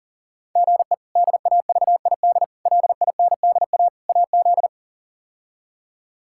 Morse code, heard as a keyed tone, about 40 wpm, 700 Hz. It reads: GE DAVID LINDA AZ
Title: K1USN SST QSO Practice - 40wpm